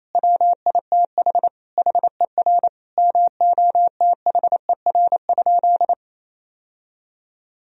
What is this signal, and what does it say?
Morse code, heard as a keyed tone, about 28 wpm, 700 Hz. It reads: WITH HER MOTHER?
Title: Sentences from Top 1000 Words - Review Part 3 - 28wpm